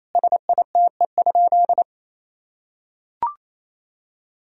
Morse code, heard as a keyed tone, about 27 wpm, 700 Hz. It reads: SITE?
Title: Sentences from Top 1000 Words - Review Part 4 - 28wpm